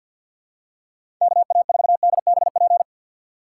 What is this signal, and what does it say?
Morse code, heard as a keyed tone, about 53 wpm, 700 Hz. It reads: KA4DBP
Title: USA Call Signs - 6 Characters - 3x Character Spacing - 50wpm